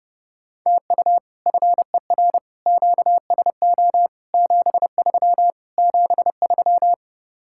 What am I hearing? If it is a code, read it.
Morse code, heard as a keyed tone, about 38 wpm, 700 Hz. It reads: TU FER QSO 73 73